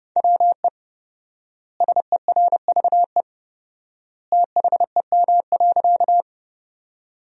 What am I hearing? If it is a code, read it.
Morse code, heard as a keyed tone, about 30 wpm, 700 Hz. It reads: WE SERVE THEM.
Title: 4x Word-Spacing - Sentences from Top 400 Words - 30wpm